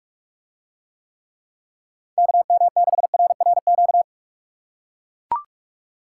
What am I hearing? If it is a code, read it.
Morse code, heard as a keyed tone, about 42 wpm, 700 Hz. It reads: KM6RRQ